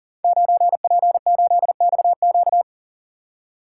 Morse code, heard as a keyed tone, about 39 wpm, 700 Hz. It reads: 9P8XQ
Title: Code Groups - 5 Letter:Numbers - 40wpm